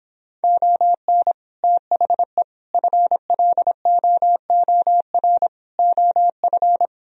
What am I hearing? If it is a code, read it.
Morse code, heard as a keyed tone, about 26 wpm, 700 Hz. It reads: ON THE FLOOR OF